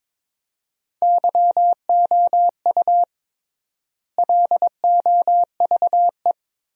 Morse code, heard as a keyed tone, about 22 wpm, 700 Hz. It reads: YOU LOVE
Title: 3x Word-Spacing - Sentences from Top 400 Words - 22wpm